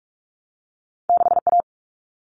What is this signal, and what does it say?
Morse code, heard as a keyed tone, about 46 wpm, 700 Hz. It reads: 6A
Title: All Cut Numbers - 2-Digit 45wpm